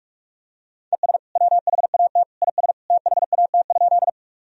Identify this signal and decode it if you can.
Morse code, heard as a keyed tone, about 46 wpm, 700 Hz. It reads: ES WHAT IS THAT?
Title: Sentences from Top 100 Words and Abbreviations 45wpm